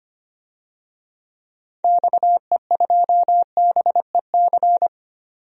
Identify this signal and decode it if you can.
Morse code, heard as a keyed tone, about 25 wpm, 700 Hz. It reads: XE2BEC
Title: DXCC-100 Call Signs - 6 Characters - 25wpm